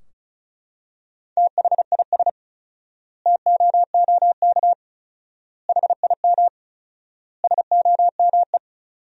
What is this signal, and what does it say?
Morse code, heard as a keyed tone, about 34 wpm, 700 Hz. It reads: THIS TOOK HIM SOME
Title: 4x Word-Spacing - Sentences from Top 300 Words - 35wpm